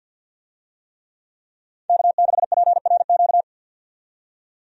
Morse code, heard as a keyed tone, about 52 wpm, 700 Hz. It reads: K6PRQ
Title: USA Call Signs - 5 Characters - 50wpm